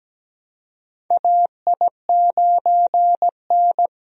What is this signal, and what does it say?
Morse code, heard as a keyed tone, about 17 wpm, 700 Hz. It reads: AI9N
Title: USA Call Signs - 4 Characters - Repeated 3x - 17wpm